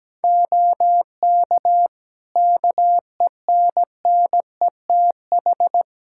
Morse code, heard as a keyed tone, about 17 wpm, 700 Hz. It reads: OK KENNETH
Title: Practice QSO - Exchange 4 - Ending - 17wpm